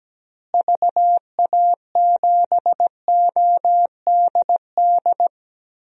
Morse code, heard as a keyed tone, about 17 wpm, 700 Hz. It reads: VA7ODD
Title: DXCC-100 Call Signs - 6 Characters - Repeated 3x - 17wpm